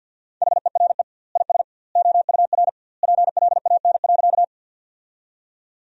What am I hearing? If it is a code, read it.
Morse code, heard as a keyed tone, about 51 wpm, 700 Hz. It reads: HERE IS OUR PLAN.
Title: 2x Word-Spacing - Sentences from Top 500 Words - 50wpm